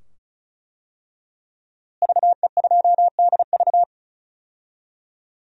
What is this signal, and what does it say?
Morse code, heard as a keyed tone, about 36 wpm, 700 Hz. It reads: VE2DV
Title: DXCC-100 Call Signs - 5 Characters - Repeated 2x - 35wpm